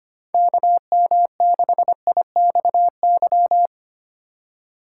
Morse code, heard as a keyed tone, about 25 wpm, 690 Hz. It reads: KM6IXY
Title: USA Call Signs - 4x Character Spacing - 25wpm